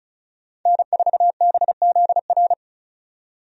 Morse code, heard as a keyed tone, about 36 wpm, 700 Hz. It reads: N4BZR
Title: USA Call Signs - 5 Characters - 35wpm